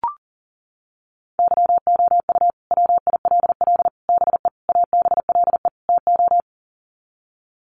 Morse code, heard as a keyed tone, about 40 wpm, 700 Hz. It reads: YOU WILL BE ABLE TO
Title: Sets of 5 Words Encore2 40wpm